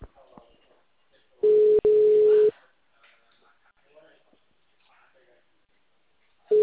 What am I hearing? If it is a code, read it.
Ringback tone (Europe, 425 Hz), with speech.
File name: out- -2007- - - .35563